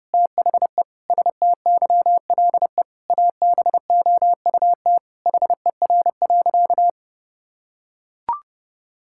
Morse code, heard as a keyed tone, about 32 wpm, 700 Hz. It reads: THE STYLE ABOUT HER.
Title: Sentences from Top 1000 Words - Review Part 5 - 30wpm